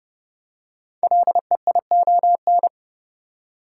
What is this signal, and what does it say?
Morse code, heard as a keyed tone, about 30 wpm, 700 Hz. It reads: LEION
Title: Code Groups - 5 Letter:Numbers - 30wpm